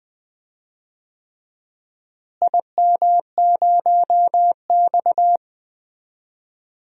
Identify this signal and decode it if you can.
Morse code, heard as a keyed tone, about 20 wpm, 700 Hz. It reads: IM0X
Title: DXCC-100 Call Signs - 4 Characters - Repeated 3x - 20wpm